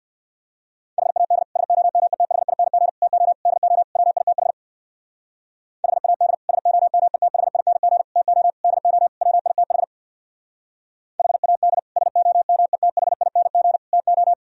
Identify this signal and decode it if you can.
Morse code, heard as a keyed tone, about 55 wpm, 700 Hz. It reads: HAD SOMETHING TO DO WITH HAD SOMETHING TO DO WITH HAD SOMETHING TO
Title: Speed-Racing - Sets of 5 Words 50wpm